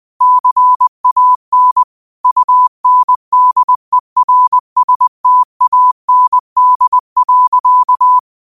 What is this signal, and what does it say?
Morse code, heard as a keyed tone, about 20 wpm, 1000 Hz. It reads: CAN UNDERSTAND.
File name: qotdfiles / 20wpm